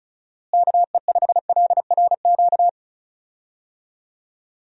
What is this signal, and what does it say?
Morse code, heard as a keyed tone, about 34 wpm, 700 Hz. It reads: KE5LRQ
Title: USA Call Signs - 6 Characters - Repeated 2x - 35wpm